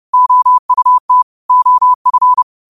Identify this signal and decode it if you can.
Morse code, heard as a keyed tone, about 30 wpm, 1000 Hz. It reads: OUT OF
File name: qotdfiles / 30wpm